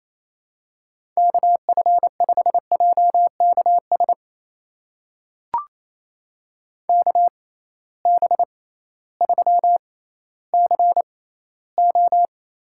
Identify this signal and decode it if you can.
Morse code, heard as a keyed tone, about 28 wpm, spaced as for 15 wpm, 700 Hz. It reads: KF5JKS KB3CO
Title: USA Call Signs - 6 Characters - 6x Character Spacing - 28wpm